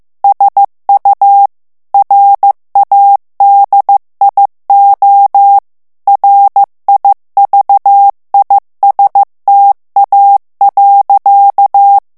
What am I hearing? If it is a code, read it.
Morse code, heard as a keyed tone, about 15 wpm, 800 Hz. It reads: SU RADIO RIVISTA.
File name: morse mp3 / ITALIEN texte clair 2 vitesse 14 mots minute